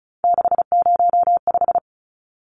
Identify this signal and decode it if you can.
Morse code, heard as a keyed tone, about 34 wpm, 700 Hz. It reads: 605
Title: Numbers - 3-Digit 35wpm